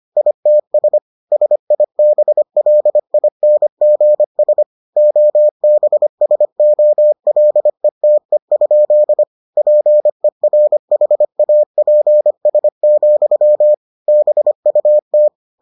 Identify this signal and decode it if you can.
Morse code, heard as a keyed tone, about 25 wpm, 600 Hz. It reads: ITS SIBLINGS OBSOLETE? PERHAPS, BUT